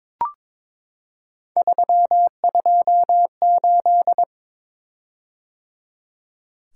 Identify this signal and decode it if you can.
Morse code, heard as a keyed tone, about 22 wpm, 700 Hz. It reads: 328
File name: practice / 22wpm / Three Letter-Number Combinations - 22wpm